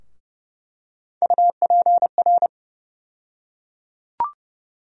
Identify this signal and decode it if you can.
Morse code, heard as a keyed tone, about 32 wpm, 700 Hz. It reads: UPR
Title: USA Call Sign Postfixes - 3 Letters 30wpm